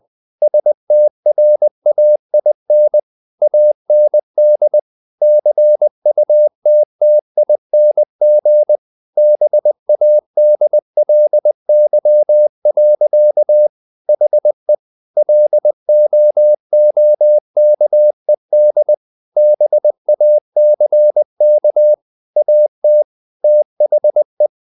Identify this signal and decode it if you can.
Morse code, heard as a keyed tone, about 20 wpm, 600 Hz. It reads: STRAIN AND CUTTING BADLY. HE LOOKED BACK AT THE
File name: the old man and the sea part 2 20wpm